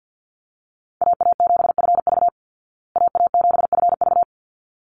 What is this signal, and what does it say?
Morse code, heard as a keyed tone, about 50 wpm, 700 Hz. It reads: AA7FV AA7FV
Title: Call Signs Repeated 3x - 50wpm